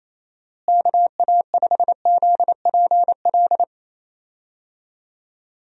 Morse code, heard as a keyed tone, about 28 wpm, 700 Hz. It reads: KA5ZPL
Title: USA Call Signs - 6 Characters - Repeated 2x - 28wpm